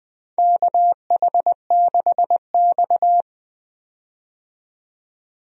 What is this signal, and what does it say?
Morse code, heard as a keyed tone, about 20 wpm, 700 Hz. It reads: KH6X
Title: DXCC-100 Call Signs - 4 Characters - 20wpm